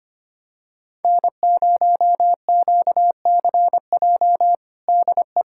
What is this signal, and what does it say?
Morse code, heard as a keyed tone, about 25 wpm, 700 Hz. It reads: N0QCJ DE
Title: K1USN SST QSO Practice 25wpm